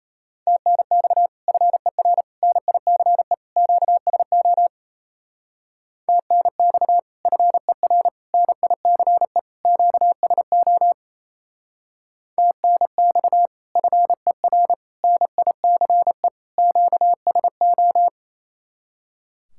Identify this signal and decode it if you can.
Morse code, heard as a keyed tone, about 34 wpm, 700 Hz. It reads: TNX FER NICE QSO TNX FER NICE QSO TNX FER NICE QSO
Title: Speed-Racing Common QSO Phrases - 28wpm